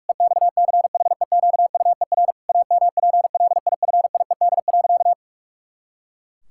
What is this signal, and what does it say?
Morse code, heard as a keyed tone, about 43 wpm, 700 Hz. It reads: EXCHEQUER AMPLIFIED.
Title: Pangram Challenge v2 - 45wpm